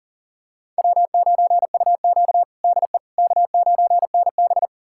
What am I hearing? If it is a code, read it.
Morse code, heard as a keyed tone, about 38 wpm, 700 Hz. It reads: W9UQ DE K9NB
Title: USA Callsign DE Callsign - 4 Characters - 40wpm